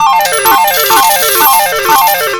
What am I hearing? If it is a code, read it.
There is no code.